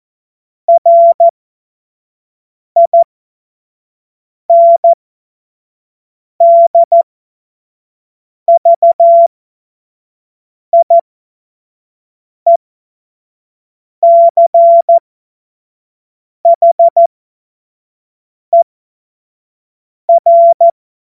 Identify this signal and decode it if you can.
Morse code, heard as a keyed tone, about 14 wpm, spaced as for 5 wpm, 680 Hz. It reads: RINDVIECHER
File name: Witz-09-September-2025-25-F70